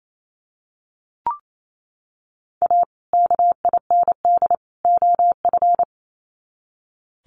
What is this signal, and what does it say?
Morse code, heard as a keyed tone, about 28 wpm, 700 Hz. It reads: A KIND OF
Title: Sets of 3 Words 28wpm